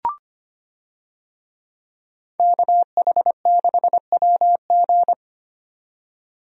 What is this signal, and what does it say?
Morse code, heard as a keyed tone, about 25 wpm, 700 Hz. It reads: KH6WG
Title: DXCC-100 Call Signs - 5 Characters - 25wpm